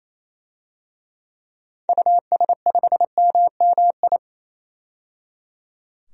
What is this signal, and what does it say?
Morse code, heard as a keyed tone, about 28 wpm, 700 Hz. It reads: US5MMI
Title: DXCC-100 Call Signs - 6 Characters - 28wpm